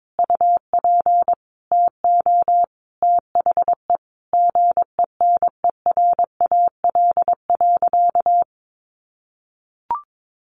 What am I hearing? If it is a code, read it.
Morse code, heard as a keyed tone, about 22 wpm, 700 Hz. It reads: UP TO THE GENERAL.
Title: Sentences from Top 1500 Words 22wpm